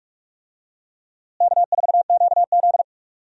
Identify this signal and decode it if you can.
Morse code, heard as a keyed tone, about 46 wpm, 700 Hz. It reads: K4QZ
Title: USA Call Signs - 4 Characters - Repeated 2x - 45wpm